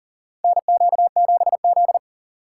Morse code, heard as a keyed tone, about 40 wpm, 700 Hz. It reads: NQ7Z
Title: USA Call Signs - 4 Characters - 4x Character Spacing - 40wpm